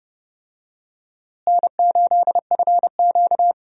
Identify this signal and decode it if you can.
Morse code, heard as a keyed tone, about 30 wpm, 700 Hz. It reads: N8FQ
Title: USA Call Signs - 4 Characters - Repeated 3x - 30wpm